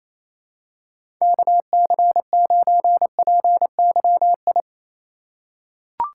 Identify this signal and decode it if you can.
Morse code, heard as a keyed tone, about 28 wpm, 700 Hz. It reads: KC9PYI
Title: USA Call Signs - 6 Characters - 4x Character Spacing - 28wpm